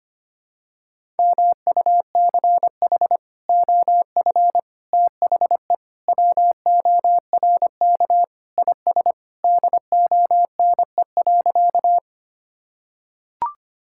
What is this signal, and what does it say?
Morse code, heard as a keyed tone, about 25 wpm, 700 Hz. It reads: MUCH OF THE WORK IS DONE.
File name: practice / 25wpm / Sentences with 3-Word Phrases - 25wpm